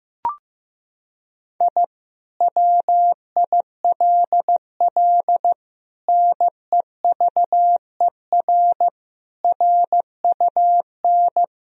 Morse code, heard as a keyed tone, about 16 wpm, 700 Hz. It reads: I WILL NEVER RUN
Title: Sentences from Top 1000 Words - Review Part 1 - 15wpm